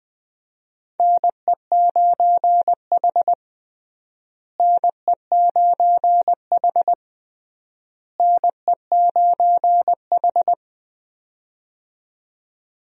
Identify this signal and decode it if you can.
Morse code, heard as a keyed tone, about 20 wpm, 700 Hz. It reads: NE9H NE9H NE9H
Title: USA Call Signs - 4 Characters - Repeated 3x - 20wpm